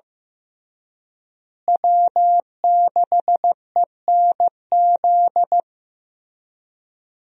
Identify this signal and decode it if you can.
Morse code, heard as a keyed tone, about 15 wpm, 700 Hz. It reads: W6ENZ
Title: USA Call Signs - 5 Characters - Repeated 3x - 15wpm